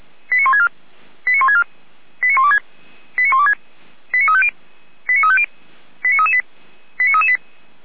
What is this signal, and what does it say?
Five-tone selective call (ZVEI), heard as quick runs of five tones, tones of about 70 ms. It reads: 89164 89164 89126 89126 89350 89350 89338 89338